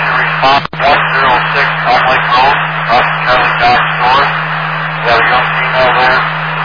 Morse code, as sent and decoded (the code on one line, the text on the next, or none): none